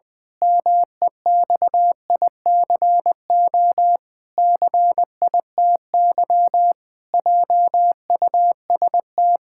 Morse code, heard as a keyed tone, about 20 wpm, 700 Hz. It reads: MEXICO CITY JUST
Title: Pangram Challenge v2 - 20wpm